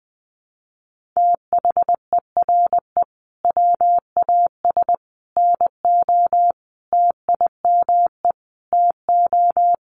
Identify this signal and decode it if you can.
Morse code, heard as a keyed tone, about 20 wpm, 700 Hz. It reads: THERE WAS NO TIME TO
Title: Sentences from Top 400 Words 20wpm